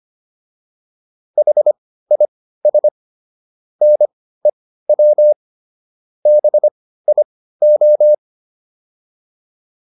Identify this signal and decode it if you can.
Morse code, heard as a keyed tone, about 25 wpm, spaced as for 15 wpm, 600 Hz. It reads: HIS NEW BIO